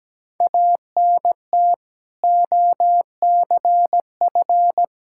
Morse code, heard as a keyed tone, about 17 wpm, 700 Hz. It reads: ANT OCF
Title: K1USN SST QSO Practice 17wpm